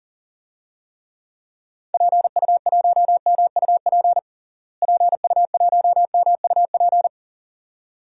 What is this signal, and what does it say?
Morse code, heard as a keyed tone, about 40 wpm, 700 Hz. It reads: PU1MUP PU1MUP
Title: DXCC-100 Call Signs - 6 Characters - Repeated 2x - 40wpm